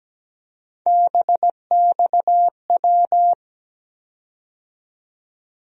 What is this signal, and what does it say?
Morse code, heard as a keyed tone, about 17 wpm, 700 Hz. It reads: BXW
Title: USA Call Sign Postfixes - 3 Letters 17wpm